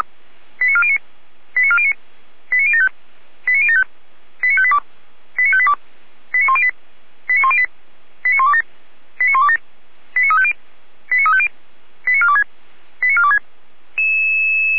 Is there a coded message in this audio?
Yes: five-tone selective call (ZVEI) from 0.6 to 13.4 s, heard as quick runs of five tones, tones of about 80 ms. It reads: 89409 89409 89075 89075 89582 89582 89108 89108 89127 89127 89350 89350 89426 89426